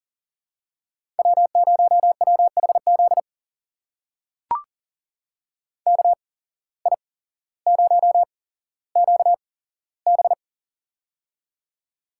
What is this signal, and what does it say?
Morse code, heard as a keyed tone, about 40 wpm, spaced as for 17 wpm, 700 Hz. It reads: W0WHZ KI0QB
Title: USA Call Signs - 5 Characters - 8x Character Spacing - 40wpm